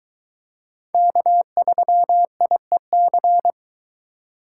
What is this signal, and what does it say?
Morse code, heard as a keyed tone, about 22 wpm, 700 Hz. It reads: K3IEC